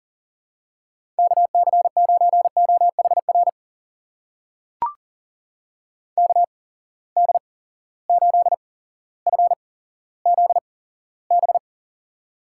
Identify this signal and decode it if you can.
Morse code, heard as a keyed tone, about 40 wpm, spaced as for 17 wpm, 700 Hz. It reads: KC9OHR KD8FZB
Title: USA Call Signs - 6 Characters - 8x Character Spacing - 40wpm